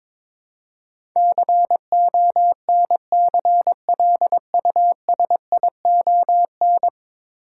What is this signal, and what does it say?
Morse code, heard as a keyed tone, about 22 wpm, 700 Hz. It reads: CONCLUSION
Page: morse-code-ninja-practice-22wpm